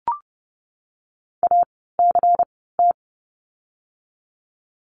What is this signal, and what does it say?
Morse code, heard as a keyed tone, about 30 wpm, spaced as for 17 wpm, 700 Hz. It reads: ACT
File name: Lesson 16f - Letter c - Words 30wpm